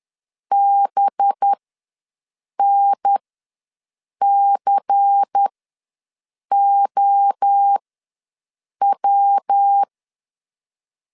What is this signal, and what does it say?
Morse code, heard as a keyed tone, about 10 wpm, spaced as for 6 wpm, 800 Hz. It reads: BNCOW
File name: LEZIONI / 5 LBVC / 5 LBVC ws 10wpm 3733